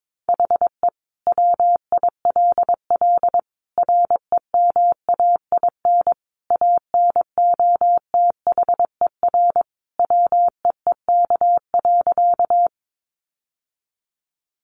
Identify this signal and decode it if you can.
Morse code, heard as a keyed tone, about 22 wpm, 700 Hz. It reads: HE WILL REMAIN ANOTHER WEEK.
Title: Sentences from Top 1200 Words 22wpm